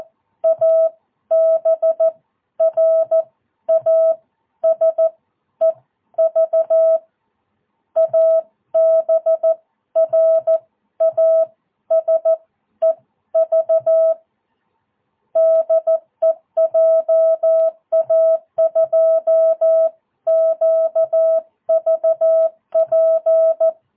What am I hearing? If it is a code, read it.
Morse code, heard as a keyed tone, about 14 wpm, spaced as for 11 wpm, 650 Hz. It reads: ABRASEV ABRASEV DE JA2QVP
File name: jh8cbh / 2022cw